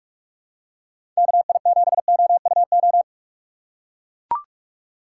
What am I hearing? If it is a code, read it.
Morse code, heard as a keyed tone, about 44 wpm, 700 Hz. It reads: KI7OUO
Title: USA Call Signs - 6x Character Spacing - 45wpm